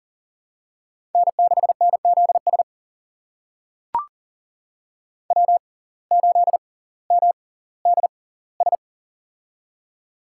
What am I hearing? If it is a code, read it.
Morse code, heard as a keyed tone, about 40 wpm, spaced as for 20 wpm, 700 Hz. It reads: N6NZS W8MDS
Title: USA Call Signs - 6x Character Spacing - 40wpm